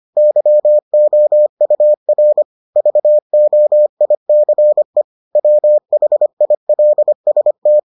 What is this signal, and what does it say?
Morse code, heard as a keyed tone, about 25 wpm, 600 Hz. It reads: YOUR VOICE WHILST